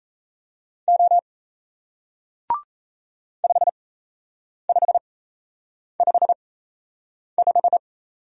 Morse code, heard as a keyed tone, about 37 wpm, spaced as for 8 wpm, 700 Hz. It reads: O 5555